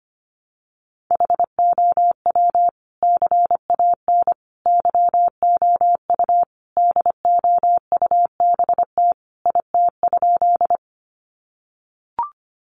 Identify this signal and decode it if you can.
Morse code, heard as a keyed tone, about 25 wpm, 700 Hz. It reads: HOW CAN YOU DOUBT IT?
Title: Sentences from Top 1500 Words 25wpm